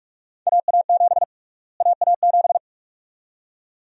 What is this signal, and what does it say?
Morse code, heard as a keyed tone, about 46 wpm, 700 Hz. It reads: AA7 AA7
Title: USA Call Sign Prefixes - Repeated 2x - 45wpm